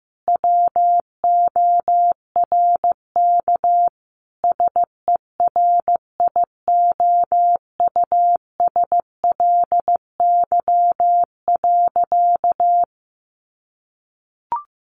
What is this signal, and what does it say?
Morse code, heard as a keyed tone, about 15 wpm, 700 Hz. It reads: WORK SERIOUSLY.
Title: Sentences from Top 1200 Words 15wpm